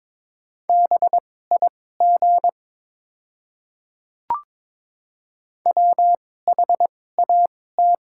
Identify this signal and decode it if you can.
Morse code, heard as a keyed tone, about 23 wpm, spaced as for 16 wpm, 700 Hz. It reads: BIG WHAT
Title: 2x Character Spacing - Top 200 Words - 22wpm